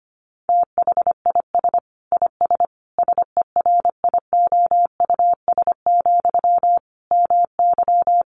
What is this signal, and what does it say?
Morse code, heard as a keyed tone, about 25 wpm, 700 Hz. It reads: THIS IS SERIOUS, MY
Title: Sentences from Top 1100 Words 25wpm